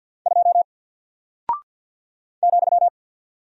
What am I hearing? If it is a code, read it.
Morse code, heard as a keyed tone, about 48 wpm, 700 Hz. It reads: 2 ,
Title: Single Character (Letter:Number:Punctuation:Prosign) Alt 50wpm